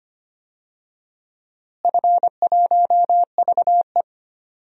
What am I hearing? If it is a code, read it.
Morse code, heard as a keyed tone, about 25 wpm, 700 Hz. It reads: F1VE
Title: DXCC-100 Call Signs - 4 Characters - Repeated 2x - 25wpm